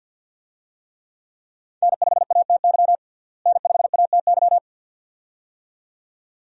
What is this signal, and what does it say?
Morse code, heard as a keyed tone, about 49 wpm, 700 Hz. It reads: N5ATY N5ATY
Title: USA Call Signs - 5 Characters - Repeated 2x - 50wpm